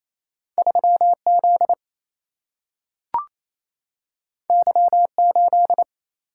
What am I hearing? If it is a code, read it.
Morse code, heard as a keyed tone, about 28 wpm, 700 Hz. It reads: 3Z Y8
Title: Single-Letter Single-Number 28wpm